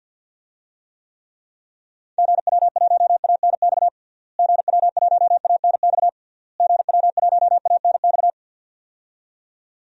Morse code, heard as a keyed tone, about 52 wpm, 700 Hz. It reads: GW1ANX GW1ANX GW1ANX
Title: DXCC-100 Call Signs - 6 Characters - Repeated 3x - 50wpm